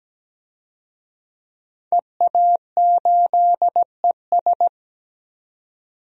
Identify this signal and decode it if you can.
Morse code, heard as a keyed tone, about 17 wpm, 700 Hz. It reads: EA8ES
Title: DXCC-100 Call Signs - 5 Characters - 17wpm